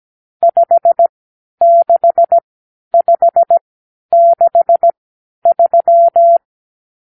Morse code, heard as a keyed tone, about 17 wpm, spaced as for 10 wpm, 680 Hz. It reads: 56563
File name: arrl / Review (1-0)